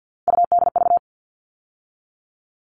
Morse code, heard as a keyed tone, about 50 wpm, 700 Hz. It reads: UDV
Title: All Cut Numbers - 3-Digit 50wpm